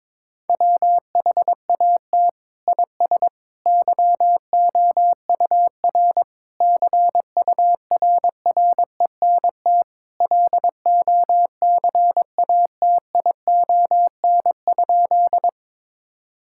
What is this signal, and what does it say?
Morse code, heard as a keyed tone, about 22 wpm, 700 Hz. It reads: WHAT IS YOUR CURRENT LOCATION?
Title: Sentences from Top 1000 Words - Review Part 6 - 22wpm